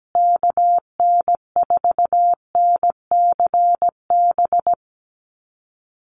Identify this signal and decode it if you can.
Morse code, heard as a keyed tone, about 17 wpm, 700 Hz. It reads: KN4NCB
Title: USA Call Signs Repeated 3x 17wpm